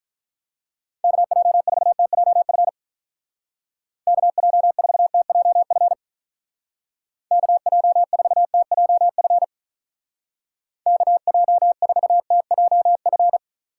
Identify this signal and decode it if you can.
Morse code, heard as a keyed tone, about 44 wpm, 700 Hz. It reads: KJ4TJF KJ4TJF KJ4TJF KJ4TJF
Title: Speed Racing - USA Call Signs 35wpm